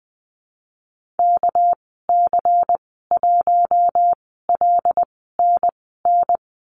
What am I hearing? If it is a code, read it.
Morse code, heard as a keyed tone, about 20 wpm, spaced as for 15 wpm, 700 Hz. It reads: KC1LNN
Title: 207 Lesson 33g - Letter j - Call Signs 20wpm